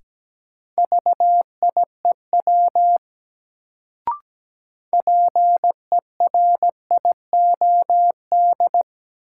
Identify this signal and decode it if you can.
Morse code, heard as a keyed tone, about 17 wpm, 700 Hz. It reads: VIEW PERIOD
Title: Sets of 5 Words - New Word Drill - 17wpm